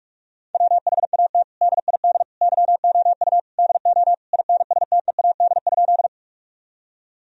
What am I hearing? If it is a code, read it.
Morse code, heard as a keyed tone, about 47 wpm, 700 Hz. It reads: WHAT DID YOU DO INSTEAD?
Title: Sentences from Top 1000 Words - Review Part 4 - 45wpm